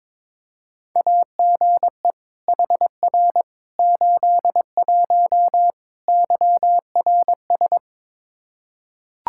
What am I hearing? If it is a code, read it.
Morse code, heard as a keyed tone, about 22 wpm, 700 Hz. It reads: AGE HR 81 YRS
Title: Practice QSO - Exchange 3 - More Info - 22wpm